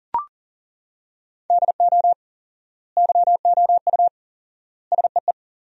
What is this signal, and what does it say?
Morse code, heard as a keyed tone, about 40 wpm, 700 Hz. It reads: DO YOU SEE
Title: 4x Word-Spacing - Sentences from Top 100 Words - 40wpm